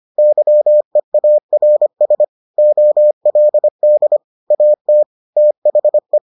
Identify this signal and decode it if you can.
Morse code, heard as a keyed tone, about 25 wpm, 600 Hz. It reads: YEARS OLD AT THE